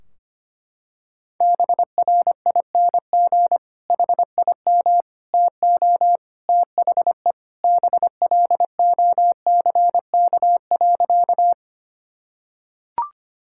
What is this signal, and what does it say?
Morse code, heard as a keyed tone, about 25 wpm, 700 Hz. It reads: BRING HIM TO THE BLOCK.
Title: Sentences from Top 1600 Words 25wpm